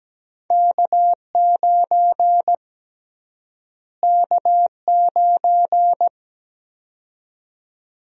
Morse code, heard as a keyed tone, about 17 wpm, 700 Hz. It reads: K9 K9
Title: USA Call Sign Prefixes - Repeated 2x - 17wpm